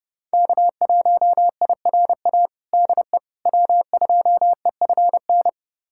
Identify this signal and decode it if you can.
Morse code, heard as a keyed tone, about 30 wpm, 700 Hz. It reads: K1IRA DE W2EFN
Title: USA Callsign DE Callsign - 5 Characters - 30wpm